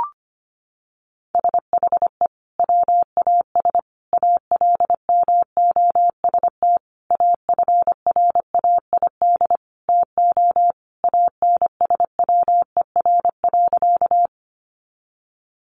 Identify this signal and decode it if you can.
Morse code, heard as a keyed tone, about 25 wpm, 700 Hz. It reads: SHE WAS ALMOST AFRAID TO ANSWER.
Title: Sentences from Top 1500 Words 25wpm